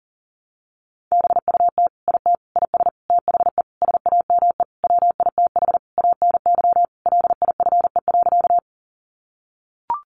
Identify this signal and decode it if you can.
Morse code, heard as a keyed tone, about 40 wpm, 700 Hz. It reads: BUT IT IS THE SAME WITH ANY LIFE.
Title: Sentences from Top 300 Words 40wpm